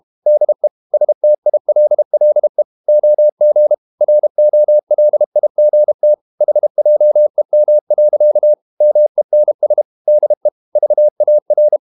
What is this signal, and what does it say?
Morse code, heard as a keyed tone, about 32 wpm, 600 Hz. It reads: DE STILLE OG ROLIGT HJEM. MENS DE VAR
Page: Ferie 32 wpm